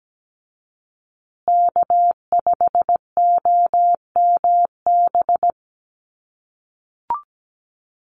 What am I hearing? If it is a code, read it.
Morse code, heard as a keyed tone, about 17 wpm, 700 Hz. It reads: K5OMB
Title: USA Call Signs Repeated 2x 17wpm